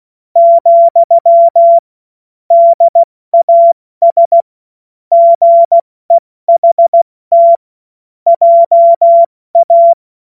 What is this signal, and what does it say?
Morse code, heard as a keyed tone, about 16 wpm, 680 Hz. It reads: , DAS GEHT JA
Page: cw-witze